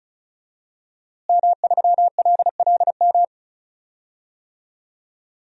Morse code, heard as a keyed tone, about 36 wpm, 700 Hz. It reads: M3LLM
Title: DXCC-100 Call Signs - 5 Characters - Repeated 2x - 35wpm